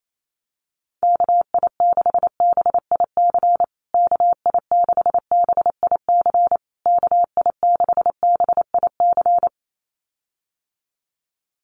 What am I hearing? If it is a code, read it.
Morse code, heard as a keyed tone, about 28 wpm, 700 Hz. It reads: KI6BIC KI6BIC KI6BIC
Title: USA Call Signs Repeated 3x 28wpm